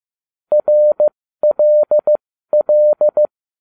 Morse code, heard as a keyed tone, about 15 wpm, spaced as for 13 wpm, 600 Hz. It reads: RLL